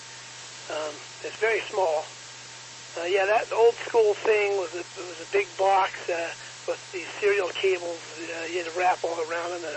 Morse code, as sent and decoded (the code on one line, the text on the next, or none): none